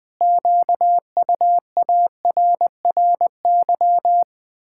Morse code, heard as a keyed tone, about 20 wpm, 700 Hz. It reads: QUARRY
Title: Pangram Challenge v2 - 20wpm